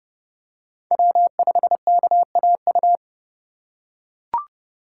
Morse code, heard as a keyed tone, about 30 wpm, 700 Hz. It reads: W5KAU
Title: USA Call Signs - 2x Character Spacing - 30wpm